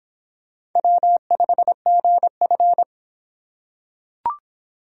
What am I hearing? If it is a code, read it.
Morse code, heard as a keyed tone, about 24 wpm, 700 Hz. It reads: W5GF